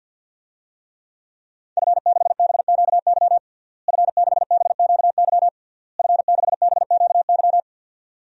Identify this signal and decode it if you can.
Morse code, heard as a keyed tone, about 52 wpm, 700 Hz. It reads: F6BQY F6BQY F6BQY
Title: DXCC-100 Call Signs - 5 Characters - Repeated 3x - 50wpm